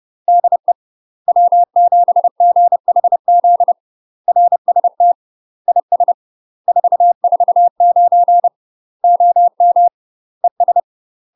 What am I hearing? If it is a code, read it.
Morse code, heard as a keyed tone, about 30 wpm, 700 Hz. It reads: DE W7GHZ RST IS 449 OM ES
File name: UK MASTER QSOs TRAINING / 30WPM / UK MASTER QSOs 0030